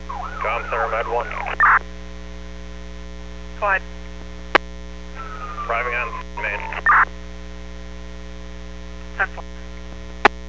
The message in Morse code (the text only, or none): none